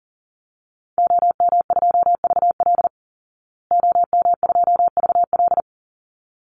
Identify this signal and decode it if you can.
Morse code, heard as a keyed tone, about 40 wpm, 700 Hz. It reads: OM2VL OM2VL
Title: Call Signs Repeated 3x - 40wpm